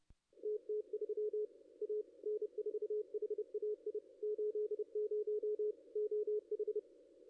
Morse code, heard as a keyed tone, about 31 wpm, 440 Hz. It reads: TT3 A N4HAI 80 OH
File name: contests / ss-cw-k5tr-op-n5rz -2200